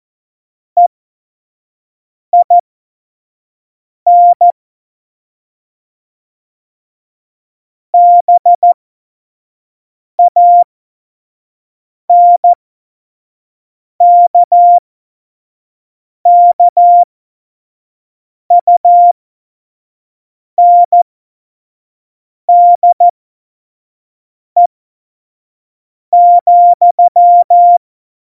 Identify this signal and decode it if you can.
Morse code, heard as a keyed tone, about 14 wpm, spaced as for 5 wpm, 700 Hz. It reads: EIN BANKKUNDE,
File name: Morsekurs / Witze / Witz 21 April 2020, 25 F70